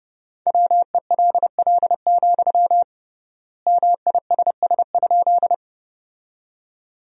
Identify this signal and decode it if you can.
Morse code, heard as a keyed tone, about 32 wpm, 700 Hz. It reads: WELL, MISS?
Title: 3x Word-Spacing - Sentences from Top 500 Words - 30wpm